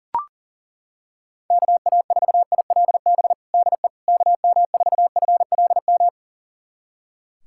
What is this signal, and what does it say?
Morse code, heard as a keyed tone, about 40 wpm, 700 Hz. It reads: KA4ILB DE KM4FLM
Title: USA Callsign DE Callsign 40wpm